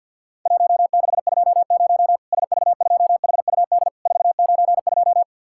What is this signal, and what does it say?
Morse code, heard as a keyed tone, about 47 wpm, 700 Hz. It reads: 1620 SV2HUD 492
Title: Worked All Europe DX Contest - QTC Exchange - 50wpm